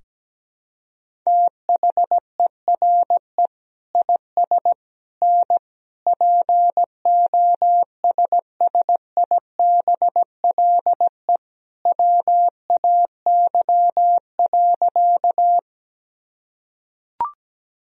Morse code, heard as a keyed tone, about 17 wpm, 700 Hz. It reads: THERE IS N POSSIBLE WAY.
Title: Sentences from Top 500 Words and Abbreviations 17wpm